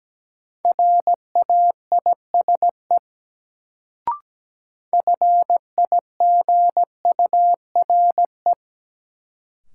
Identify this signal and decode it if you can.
Morse code, heard as a keyed tone, about 17 wpm, 700 Hz. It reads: RAISE FIGURE
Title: Sentences from Top 1400 Words - New Word Drill Alt - 17wpm